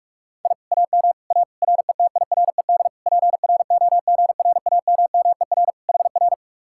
Morse code, heard as a keyed tone, about 44 wpm, 700 Hz. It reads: I AM A RETIRED PROGRAMMER HR